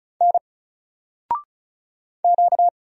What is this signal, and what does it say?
Morse code, heard as a keyed tone, about 36 wpm, 700 Hz. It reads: N Q
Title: Single Character (Letter:Number:Punctuation:Prosign) 35wpm